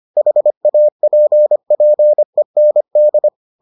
Morse code, heard as a keyed tone, about 25 wpm, 600 Hz. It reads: HAPPEND